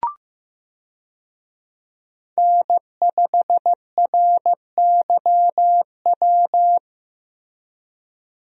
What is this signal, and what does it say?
Morse code, heard as a keyed tone, about 15 wpm, 700 Hz. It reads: N5RYW
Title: USA Call Signs - 5 Characters - 15wpm